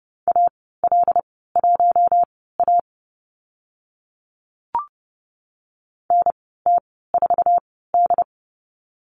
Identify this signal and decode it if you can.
Morse code, heard as a keyed tone, about 30 wpm, spaced as for 17 wpm, 700 Hz. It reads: AL1A NT4D
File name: Lesson 15g - Number 5 - Call Signs 30wpm